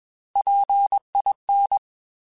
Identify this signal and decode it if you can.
Morse code, heard as a keyed tone, about 21 wpm, 800 Hz. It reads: PIN
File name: Alphabetical / Alphabetical mp3s / Ps @20